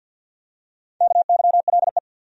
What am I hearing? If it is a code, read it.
Morse code, heard as a keyed tone, about 52 wpm, 700 Hz. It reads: KYLE
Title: USA Common Names - 50wpm